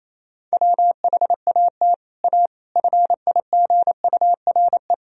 Morse code, heard as a keyed tone, about 28 wpm, 700 Hz. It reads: WHAT A FIGURE
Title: Sentences from Top 1000 Words - Review Part 2 - 28wpm